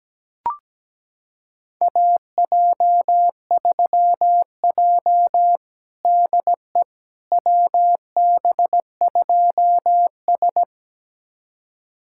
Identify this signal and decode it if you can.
Morse code, heard as a keyed tone, about 17 wpm, 700 Hz. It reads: AJ3J DE WB2S
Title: USA Callsign DE Callsign - 4 Characters - 17wpm